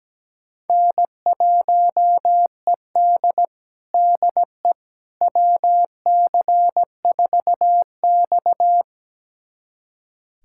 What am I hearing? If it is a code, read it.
Morse code, heard as a keyed tone, about 17 wpm, 700 Hz. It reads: N1ED DE WC4X
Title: USA Callsign DE Callsign - 4 Characters - 17wpm